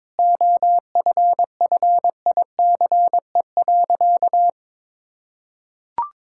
Morse code, heard as a keyed tone, about 22 wpm, 700 Hz. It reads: OFFICE.
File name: practice / 22wpm / Sentences with 2-Word Phrases - 22wpm